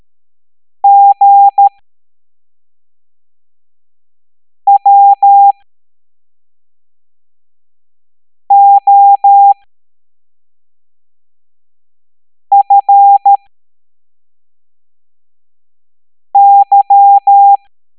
Morse code, heard as a keyed tone, about 13 wpm, spaced as for 3 wpm, 800 Hz. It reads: GWOFY